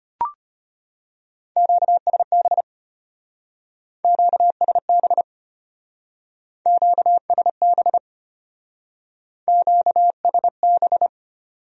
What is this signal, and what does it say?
Morse code, heard as a keyed tone, about 31 wpm, 700 Hz. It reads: QSB QSB QSB QSB
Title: Speed-Racing Q-Codes - 25wpm